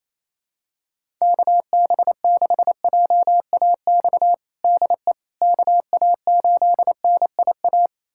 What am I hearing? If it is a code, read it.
Morse code, heard as a keyed tone, about 28 wpm, 700 Hz. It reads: KB6JAX DE KA8NIA
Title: USA Callsign DE Callsign - 6 Characters - 28wpm